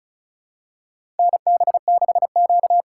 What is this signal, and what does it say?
Morse code, heard as a keyed tone, about 36 wpm, 700 Hz. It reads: NB6Q
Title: USA Call Signs - 4 Characters - 2x Character Spacing - 35wpm